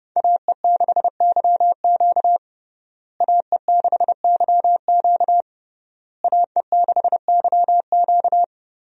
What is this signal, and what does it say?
Morse code, heard as a keyed tone, about 30 wpm, 700 Hz. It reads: AE6YQ AE6YQ AE6YQ
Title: USA Call Signs - 5 Characters - Repeated 3x - 30wpm